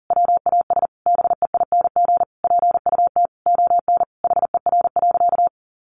Morse code, heard as a keyed tone, about 40 wpm, 700 Hz. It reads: WAS BEING PUT ON HER.
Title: Sentences from Top 1300 Words 40wpm